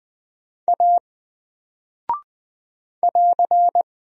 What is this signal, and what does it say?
Morse code, heard as a keyed tone, about 20 wpm, 700 Hz. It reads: A <AR>
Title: Single Character (Letter:Number:Punctuation:Prosign) Alt 20wpm